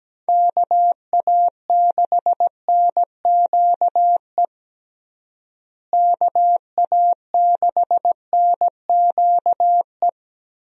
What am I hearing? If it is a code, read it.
Morse code, heard as a keyed tone, about 17 wpm, 700 Hz. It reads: KA6NQE KA6NQE
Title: USA Call Signs - 6 Characters - Repeated 3x - 17wpm